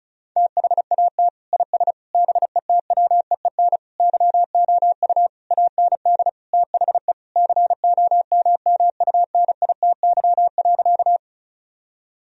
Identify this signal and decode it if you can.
Morse code, heard as a keyed tone, about 36 wpm, 700 Hz. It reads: THAT IS BETWEEN YOU AND THE COMMUNITY.
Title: Sentences from Top 1000 Words - Review Part 4 - 35wpm